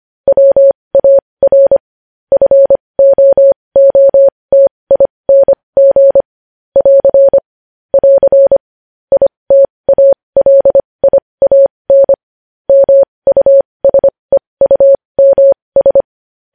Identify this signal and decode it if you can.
Morse code, heard as a keyed tone, about 25 wpm, spaced as for 20 wpm, 550 Hz. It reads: WAR FOOTING <AR> <AR> ITALIAN MUSEUMS
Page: News headlines in Morse code at 20 WPM